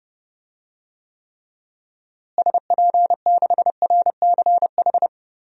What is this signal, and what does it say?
Morse code, heard as a keyed tone, about 30 wpm, 700 Hz. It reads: SP6RCH
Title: DXCC-100 Call Signs - 6 Characters - Repeated 3x - 30wpm